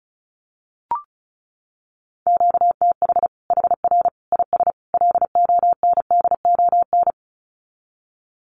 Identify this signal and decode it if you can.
Morse code, heard as a keyed tone, about 34 wpm, 700 Hz. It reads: QTH HR IS LONDON
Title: Common QSO Phrases 35wpm